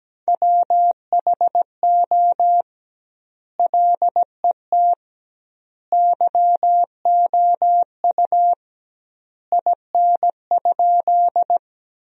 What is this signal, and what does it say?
Morse code, heard as a keyed tone, about 17 wpm, 700 Hz. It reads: WHO LET YOU IN?
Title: 2x Word-Spacing - Sentences from Top 300 Words - 17wpm